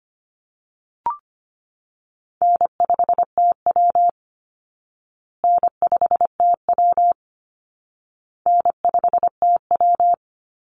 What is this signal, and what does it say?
Morse code, heard as a keyed tone, about 25 wpm, 700 Hz. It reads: N5TW N5TW N5TW
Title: Easy Call Signs Repeated 3x - 25wpm